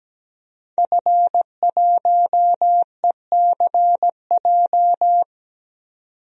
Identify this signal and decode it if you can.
Morse code, heard as a keyed tone, about 17 wpm, 700 Hz. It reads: F1ECJ
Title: DXCC-100 Call Signs - 5 Characters - 17wpm